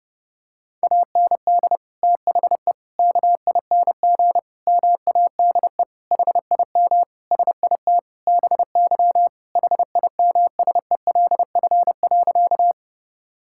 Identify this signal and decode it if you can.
Morse code, heard as a keyed tone, about 30 wpm, 700 Hz. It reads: AND THE KING MADE HIM SIT BY HIMSELF.
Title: Sentences from Top 1600 Words 30wpm